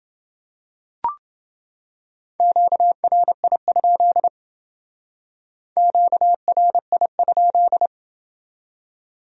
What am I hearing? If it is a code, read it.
Morse code, heard as a keyed tone, about 29 wpm, 700 Hz. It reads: QRI? QRI?